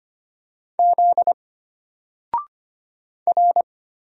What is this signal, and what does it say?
Morse code, heard as a keyed tone, about 25 wpm, 700 Hz. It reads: Z R
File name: Single-Letter-Number-25wpm